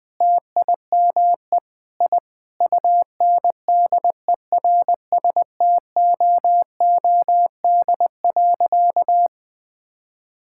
Morse code, heard as a keyed tone, about 20 wpm, 700 Hz. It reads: TIME I UNDERSTOOD.
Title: Sentences with 5-Word Phrases - Spoken Beginning - 20wpm